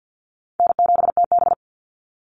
Morse code, heard as a keyed tone, about 50 wpm, 700 Hz. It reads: N7TB
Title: Call Signs - 50wpm